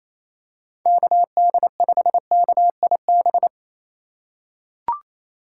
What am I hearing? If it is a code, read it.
Morse code, heard as a keyed tone, about 28 wpm, 700 Hz. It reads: KD5KIB
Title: USA Call Signs - 6 Characters - 2x Character Spacing - 28wpm